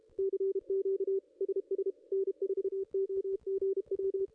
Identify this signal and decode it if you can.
Morse code, heard as a keyed tone, about 28 wpm, 410 Hz. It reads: CQ SS N4OGW
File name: contests / ss-cw-k5tr-op-n5rz -0500